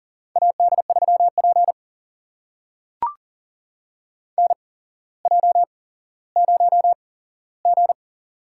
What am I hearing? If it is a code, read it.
Morse code, heard as a keyed tone, about 40 wpm, spaced as for 17 wpm, 700 Hz. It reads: AD3P NJ0G
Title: USA Call Signs - 4 Characters - 8x Character Spacing - 40wpm